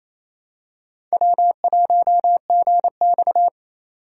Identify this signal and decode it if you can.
Morse code, heard as a keyed tone, about 28 wpm, 700 Hz. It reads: W1GX
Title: USA Call Signs - 4 Characters - 3x Character Spacing - 28wpm